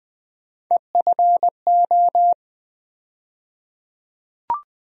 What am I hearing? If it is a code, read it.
Morse code, heard as a keyed tone, about 20 wpm, 700 Hz. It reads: EFO
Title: USA Call Sign Postfixes - 3 Letters 20wpm